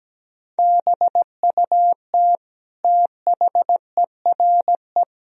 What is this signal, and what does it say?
Morse code, heard as a keyed tone, about 17 wpm, 700 Hz. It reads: BUT THERE
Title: Sentences from Top 1000 Words - Review Part 2 - 17wpm